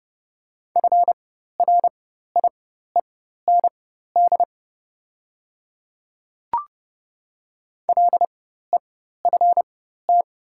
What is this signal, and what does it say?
Morse code, heard as a keyed tone, about 30 wpm, spaced as for 14 wpm, 690 Hz. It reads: FRIEND LEFT
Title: 4x Character Spacing - Top 300 Words - 30wpm